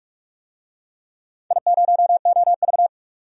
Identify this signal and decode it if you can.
Morse code, heard as a keyed tone, about 42 wpm, 700 Hz. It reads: I0OV
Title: DXCC-100 Call Signs - 4 Characters - 45wpm